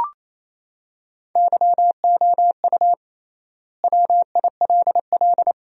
Morse code, heard as a keyed tone, about 28 wpm, 700 Hz. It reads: YOU WILL
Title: 3x Word-Spacing - Sentences from Top 400 Words - 28wpm